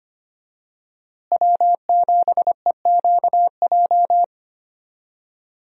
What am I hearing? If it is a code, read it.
Morse code, heard as a keyed tone, about 25 wpm, 700 Hz. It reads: W7EQJ
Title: USA Call Signs - 5 Characters - Repeated 2x - 25wpm